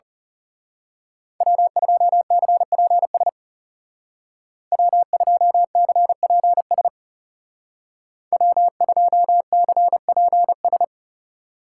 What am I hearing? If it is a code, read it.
Morse code, heard as a keyed tone, about 36 wpm, 700 Hz. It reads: W2CPS W2CPS W2CPS
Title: Speed Racing - USA 1x3 Call Signs 30wpm